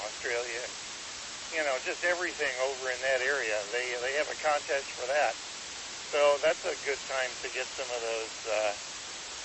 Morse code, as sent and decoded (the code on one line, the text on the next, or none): none